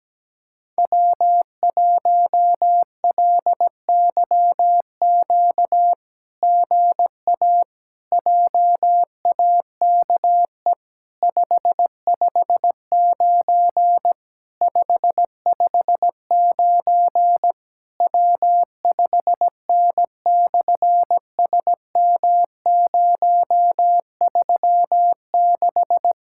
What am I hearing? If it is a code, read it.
Morse code, heard as a keyed tone, about 17 wpm, 700 Hz. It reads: W1LYQ GA JAKE 559 559 W5N/SM036
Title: SOTA QSO Practice 17wpm